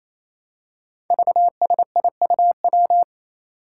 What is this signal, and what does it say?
Morse code, heard as a keyed tone, about 28 wpm, 700 Hz. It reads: VSIUW
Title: Five Challenging Letter-Number Combinations 28wpm